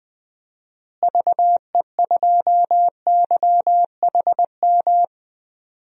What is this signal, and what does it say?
Morse code, heard as a keyed tone, about 20 wpm, 700 Hz. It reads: VE2YHM
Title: DXCC-100 Call Signs - 6 Characters - Repeated 2x - 20wpm